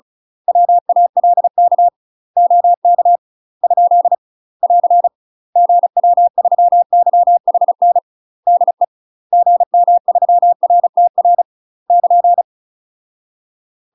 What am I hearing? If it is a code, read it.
Morse code, heard as a keyed tone, about 34 wpm, 700 Hz. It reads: WALK OK ? <AR> GW3YHN DE GM3RTR <KN>
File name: UK MASTER QSOs TRAINING/35WPM